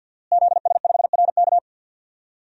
Morse code, heard as a keyed tone, about 51 wpm, 700 Hz. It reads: ZS5RK
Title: DXCC-100 Call Signs - 5 Characters - Repeated 2x - 50wpm